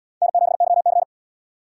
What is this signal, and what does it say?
Morse code, heard as a keyed tone, about 77 wpm, 700 Hz. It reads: N7YC